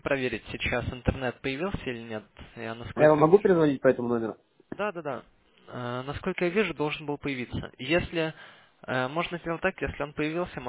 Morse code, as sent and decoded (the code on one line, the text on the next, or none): none